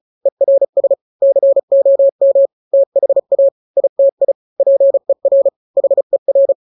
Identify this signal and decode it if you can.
Morse code, heard as a keyed tone, about 34 wpm, 550 Hz. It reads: ERS COM THA ITI PER HER
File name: 3-grams 35 30 WPM